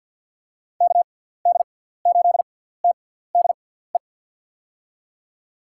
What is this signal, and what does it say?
Morse code, heard as a keyed tone, about 53 wpm, spaced as for 17 wpm, 700 Hz. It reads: KD8TDE
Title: USA Call Signs - 6 Characters - 6x Character Spacing - 50wpm